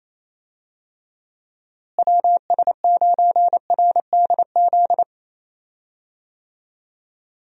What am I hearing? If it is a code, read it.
Morse code, heard as a keyed tone, about 28 wpm, 700 Hz. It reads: WS9RDZ
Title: USA Call Signs - 6 Characters - Repeated 2x - 28wpm